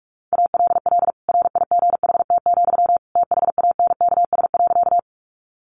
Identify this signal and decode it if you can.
Morse code, heard as a keyed tone, about 44 wpm, 700 Hz. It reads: ALL RIGHT, THANKS.
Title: Sentences from Top 600 Words 45wpm